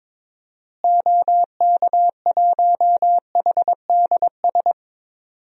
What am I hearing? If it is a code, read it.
Morse code, heard as a keyed tone, about 22 wpm, 700 Hz. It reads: OK1HDS
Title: DXCC-100 Call Signs - 6 Characters - Repeated 3x - 22wpm